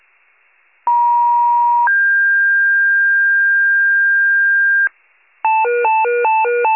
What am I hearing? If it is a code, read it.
Two-tone page, heard as a one-second tone, then a three-second tone, about 950 then 1600 Hz, with noise. It is heard from 0.9 to 4.9 s.